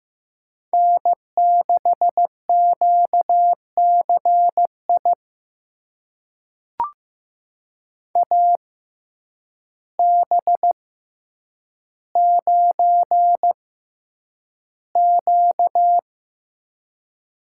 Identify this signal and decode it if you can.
Morse code, heard as a keyed tone, about 15 wpm, spaced as for 8 wpm, 700 Hz. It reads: N6QCI AB9Q
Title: USA Call Signs - 5 Characters - 6x Character Spacing - 15wpm